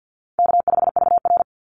Morse code, heard as a keyed tone, about 50 wpm, 700 Hz. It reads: K5VR
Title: Easy Call Signs - 50wpm